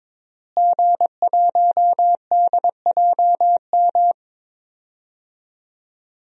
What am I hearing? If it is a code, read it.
Morse code, heard as a keyed tone, about 22 wpm, 700 Hz. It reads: G1DJM
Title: DXCC-100 Call Signs - 5 Characters - Repeated 2x - 22wpm